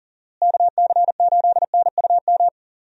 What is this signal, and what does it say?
Morse code, heard as a keyed tone, about 37 wpm, 700 Hz. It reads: KC8NUM
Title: USA Call Signs - 6 Characters - Repeated 3x - 40wpm